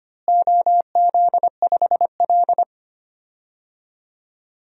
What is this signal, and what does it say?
Morse code, heard as a keyed tone, about 25 wpm, 700 Hz. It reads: OZ5L
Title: DXCC-100 Call Signs - 4 Characters - Repeated 2x - 25wpm